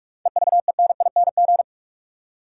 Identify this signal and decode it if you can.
Morse code, heard as a keyed tone, about 44 wpm, 700 Hz. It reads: EVENING
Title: Seven Letter Words - 45wpm